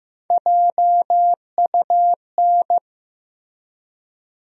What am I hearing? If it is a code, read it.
Morse code, heard as a keyed tone, about 15 wpm, 700 Hz. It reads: JUN
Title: USA Call Sign Postfixes - 3 Letters 15wpm